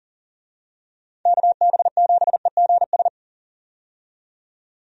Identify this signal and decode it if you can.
Morse code, heard as a keyed tone, about 40 wpm, 700 Hz. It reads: KB7EGS
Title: USA Call Signs - 6 Characters - 40wpm